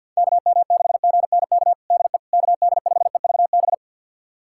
Morse code, heard as a keyed tone, about 52 wpm, 700 Hz. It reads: KM6GNK DE KD5EVB
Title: USA Callsign DE Callsign - 6 Characters - 50wpm